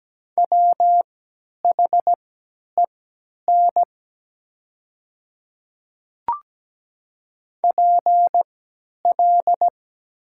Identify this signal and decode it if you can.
Morse code, heard as a keyed tone, about 17 wpm, spaced as for 10 wpm, 700 Hz. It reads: WHEN PL
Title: 3x Character Spacing - Top 300 Words - 17wpm